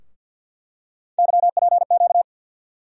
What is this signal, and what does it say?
Morse code, heard as a keyed tone, about 47 wpm, 700 Hz. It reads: YPQ
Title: USA Call Sign Postfixes - 3 Letters 50wpm